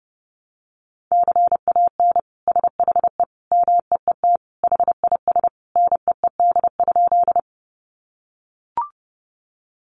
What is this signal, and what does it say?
Morse code, heard as a keyed tone, about 30 wpm, 700 Hz. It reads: CAN SHE MEET HIS NEED?
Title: Sentences from Top 1100 Words 30wpm